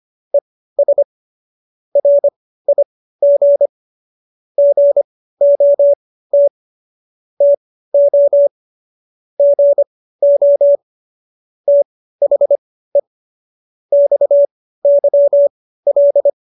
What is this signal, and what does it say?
Morse code, heard as a keyed tone, about 25 wpm, spaced as for 15 wpm, 570 Hz. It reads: ES RIG GOT TO GO THE XYL
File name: qso222 15